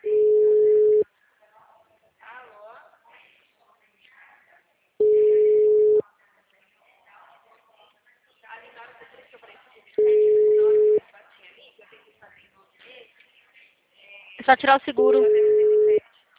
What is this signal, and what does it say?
Ringback tone (Europe, 425 Hz), with speech.
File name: out- -2039- - - .29942